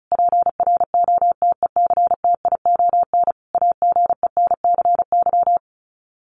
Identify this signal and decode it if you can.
Morse code, heard as a keyed tone, about 34 wpm, 700 Hz. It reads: PROTECTION AGENCY
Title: Sets of 5 Words Encore2 35wpm